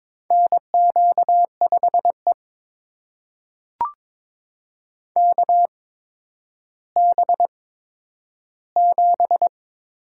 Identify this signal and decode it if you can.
Morse code, heard as a keyed tone, about 22 wpm, spaced as for 11 wpm, 700 Hz. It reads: NQ5E KB7
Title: USA Call Signs - 4 Characters - 8x Character Spacing - 22wpm